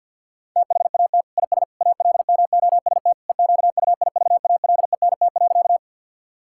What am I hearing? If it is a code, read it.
Morse code, heard as a keyed tone, about 50 wpm, 700 Hz. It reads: THAT IS ALMOST EQUIVALENT.
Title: Sentences from Top 1600 Words 50wpm